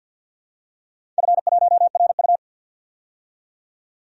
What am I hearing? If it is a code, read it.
Morse code, heard as a keyed tone, about 47 wpm, 700 Hz. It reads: F1RU
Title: DXCC-100 Call Signs - 4 Characters - 50wpm